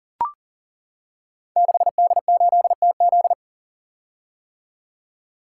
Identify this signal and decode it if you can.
Morse code, heard as a keyed tone, about 42 wpm, 700 Hz. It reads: 6D8TZ
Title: Code Groups - 5 Letter:Numbers - 40wpm